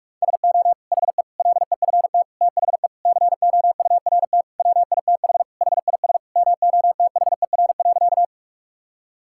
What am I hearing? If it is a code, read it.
Morse code, heard as a keyed tone, about 43 wpm, 700 Hz. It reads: SO HE LEFT THE COURT WITH HIS MOTHER.
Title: Sentences from Top 1600 Words 45wpm